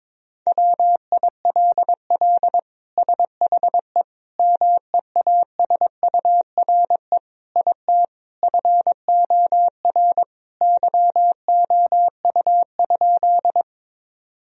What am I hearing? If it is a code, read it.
Morse code, heard as a keyed tone, about 22 wpm, 700 Hz. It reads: WILL SHE MEASURE IT FOR YOU?
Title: Sentences from Top 1000 Words - Review Part 2 - 22wpm